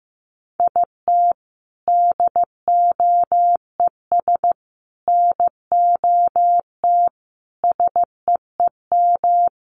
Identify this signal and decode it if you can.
Morse code, heard as a keyed tone, about 15 wpm, 700 Hz. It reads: IT DOES NOT SEEM
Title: Sets of 5 Words Encore2 15wpm